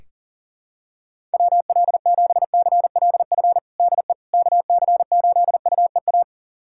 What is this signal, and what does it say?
Morse code, heard as a keyed tone, about 37 wpm, 700 Hz. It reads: WL7CLF DE KC8UEA
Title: USA Callsign DE Callsign - 6 Characters - 40wpm